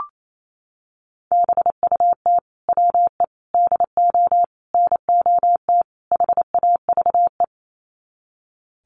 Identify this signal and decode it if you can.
Morse code, heard as a keyed tone, about 28 wpm, 700 Hz. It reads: BUT WE DO NOT HAVE
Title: Sets of 5 Words Encore2 28wpm